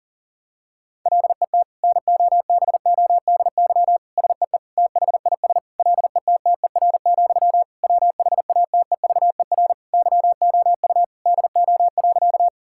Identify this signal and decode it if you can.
Morse code, heard as a keyed tone, about 40 wpm, 700 Hz. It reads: LET NOBODY SEE THIS LETTER, WHATEVER YOU DO.
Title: Sentences from Top 1600 Words 40wpm